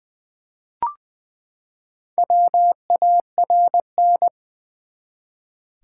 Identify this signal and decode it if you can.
Morse code, heard as a keyed tone, about 20 wpm, 700 Hz. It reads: WARN
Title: Four Letter Words - 20wpm